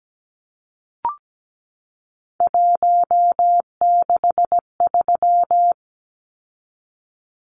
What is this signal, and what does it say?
Morse code, heard as a keyed tone, about 17 wpm, 700 Hz. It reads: 163
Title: Numbers - 3-Digit 17wpm